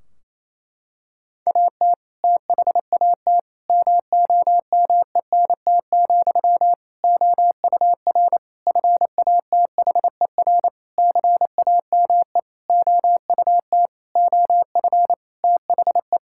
Morse code, heard as a keyed tone, about 28 wpm, 700 Hz. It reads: AT THAT MOMENT, OUR FATHER CAME OUT OF THE
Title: Sentences from Top 1600 Words 28wpm